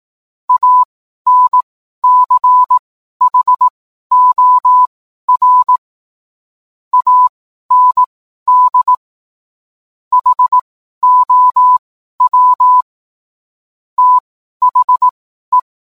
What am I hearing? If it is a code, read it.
Morse code, heard as a keyed tone, about 18 wpm, spaced as for 12 wpm, 1000 Hz. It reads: ANCHOR AND HOW THE